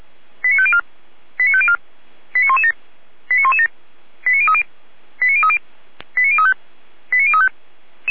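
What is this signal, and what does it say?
Five-tone selective call (ZVEI), heard as quick runs of five tones, tones of about 70 ms. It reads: 89504 89504 89117 89117 89030 89030 89035 89035
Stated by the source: ZVEI five-tone selective call, received on SDR.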